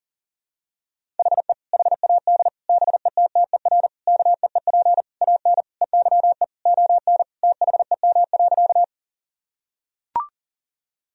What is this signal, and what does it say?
Morse code, heard as a keyed tone, about 37 wpm, 700 Hz. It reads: HE HAD BETTER KEEP AN EYE ON THEM.
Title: Sentences from Top 1000 Words - Review Part 1 - 40wpm